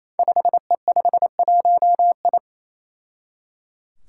Morse code, heard as a keyed tone, about 27 wpm, 700 Hz. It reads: 5E51I
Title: Five Challenging Letter-Number Combinations 28wpm